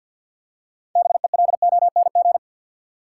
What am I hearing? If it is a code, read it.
Morse code, heard as a keyed tone, about 51 wpm, 700 Hz. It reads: BELONG
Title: Six Letter Words - 50wpm